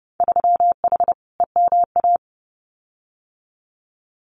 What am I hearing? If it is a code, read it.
Morse code, heard as a keyed tone, about 30 wpm, 700 Hz. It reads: 3H EMA
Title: Winter Field Day Exchanges Encore 30wpm